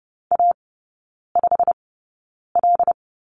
Morse code, heard as a keyed tone, about 30 wpm, spaced as for 9 wpm, 700 Hz. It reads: A5L
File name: course / 30:10wpm / 087 Lesson 15e - Number 5 - Three Characters 30wpm